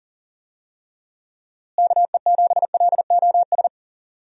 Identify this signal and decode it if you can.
Morse code, heard as a keyed tone, about 40 wpm, 700 Hz. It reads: KE7LOS
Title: USA Call Signs - 6 Characters - Repeated 3x - 40wpm